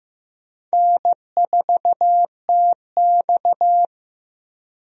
Morse code, heard as a keyed tone, about 15 wpm, 700 Hz. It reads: N4TX
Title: USA Call Signs - 4 Characters - Repeated 3x - 15wpm